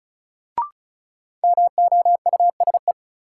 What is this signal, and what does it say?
Morse code, heard as a keyed tone, about 36 wpm, 700 Hz. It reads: MOUSE
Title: Sentences from Top 700 Words - New Word Drill - 35wpm